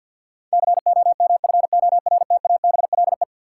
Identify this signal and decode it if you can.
Morse code, heard as a keyed tone, about 53 wpm, 700 Hz. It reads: COMFORTABLE
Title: Sentences from Top 1500 Words - New Word Drill Alt - 50wpm